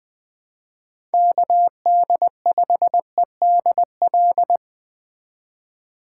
Morse code, heard as a keyed tone, about 20 wpm, 700 Hz. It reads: KD5EDL
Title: 2023 POTA Hunter Call Signs - 20wpm